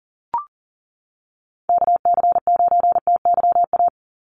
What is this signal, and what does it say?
Morse code, heard as a keyed tone, about 40 wpm, 700 Hz. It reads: KC9TYA
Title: NA SOTA Chaser Callsigns 40wpm